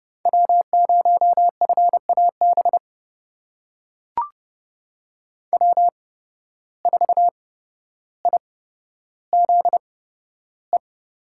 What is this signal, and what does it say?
Morse code, heard as a keyed tone, about 29 wpm, spaced as for 13 wpm, 700 Hz. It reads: W0FAB W4IZE
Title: USA Call Signs - 5 Characters - 8x Character Spacing - 30wpm